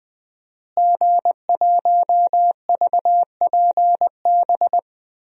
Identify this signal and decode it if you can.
Morse code, heard as a keyed tone, about 20 wpm, 700 Hz. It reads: G1VPB
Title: DXCC-100 Call Signs - 5 Characters - Repeated 3x - 20wpm